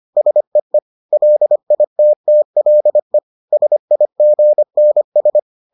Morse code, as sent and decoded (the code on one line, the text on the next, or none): ... . / .-.. .. - - .-.. / ... .. --. -. ...
SEE LITTLE SIGNS